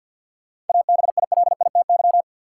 Morse code, heard as a keyed tone, about 53 wpm, 700 Hz. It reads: ABILITY
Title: Sentences from Top 1000 Words - Review Part 3 - New Word Drill - 50wpm